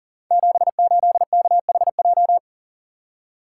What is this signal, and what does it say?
Morse code, heard as a keyed tone, about 42 wpm, 700 Hz. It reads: 78KHJ
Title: Code Groups - 5 Letter:Numbers - 40wpm